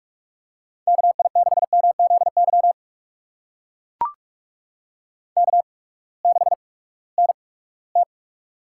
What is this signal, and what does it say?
Morse code, heard as a keyed tone, about 44 wpm, spaced as for 22 wpm, 700 Hz. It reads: KI6MZY K6NT
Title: USA Call Signs - 8x Character Spacing - 45wpm